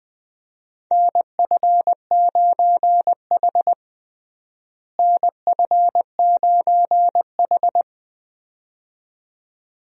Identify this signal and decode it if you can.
Morse code, heard as a keyed tone, about 20 wpm, 700 Hz. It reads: NF9H NF9H
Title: USA Call Signs - 4 Characters - Repeated 2x - 20wpm